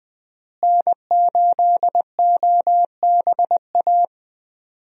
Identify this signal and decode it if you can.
Morse code, heard as a keyed tone, about 20 wpm, 700 Hz. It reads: N8OBA
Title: USA Call Signs - 5 Characters - Repeated 3x - 20wpm